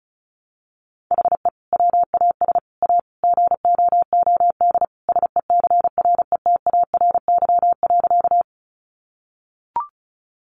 Morse code, heard as a keyed tone, about 34 wpm, 700 Hz. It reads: HE WAS A GOOD SECRETARY.
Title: Sentences from Top 1300 Words 35wpm